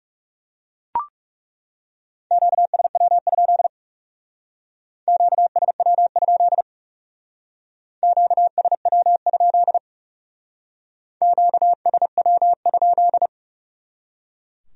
Morse code, heard as a keyed tone, about 38 wpm, 700 Hz. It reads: QSW? QSW? QSW? QSW?